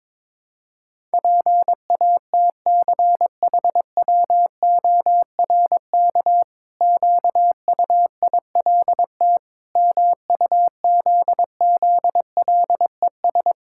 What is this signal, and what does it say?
Morse code, heard as a keyed tone, about 22 wpm, 700 Hz. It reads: PATCHWORK QUILT MUZZLES
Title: Pangram Challenge v2 - 22wpm